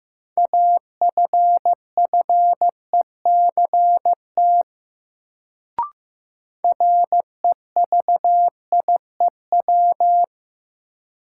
Morse code, heard as a keyed tone, about 15 wpm, 700 Hz. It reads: AFFECT REVIEW
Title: Sentences from Top 900 Words - New Word Drill - 15wpm